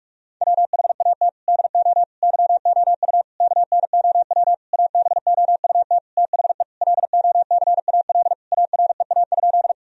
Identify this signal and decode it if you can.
Morse code, heard as a keyed tone, about 45 wpm, 700 Hz. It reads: WHAT DO YOU KNOW ABOUT THE LOCAL AREA?
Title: Sentences from Top 1000 Words - Review Part 3 - 45wpm